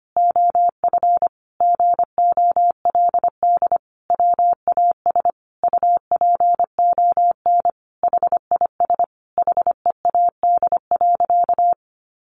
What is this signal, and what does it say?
Morse code, heard as a keyed tone, about 25 wpm, 700 Hz. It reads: OF GOLD WAS UPON HIS HEAD.
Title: Sentences from Top 1100 Words 25wpm